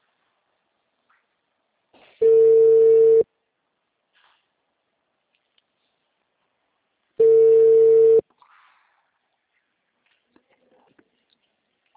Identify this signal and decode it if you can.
Ringback tone (China, 450 Hz).
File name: out- -2003- - - .64064